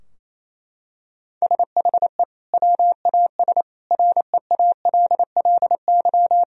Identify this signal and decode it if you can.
Morse code, heard as a keyed tone, about 28 wpm, 700 Hz. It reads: SHE WAS REALLY
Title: Sentences from Top 1000 Words - Review Part 4 - 28wpm